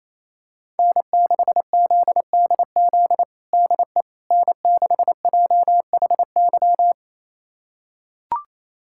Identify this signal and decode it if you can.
Morse code, heard as a keyed tone, about 28 wpm, 690 Hz. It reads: N6ZDZ DE N6JHY
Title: USA Callsign DE Callsign - 5 Characters - 28wpm